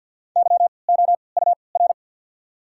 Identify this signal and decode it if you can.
Morse code, heard as a keyed tone, about 54 wpm, spaced as for 29 wpm, 690 Hz. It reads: YOUR
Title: 3x Character Spacing - Top 100 Words - 50wpm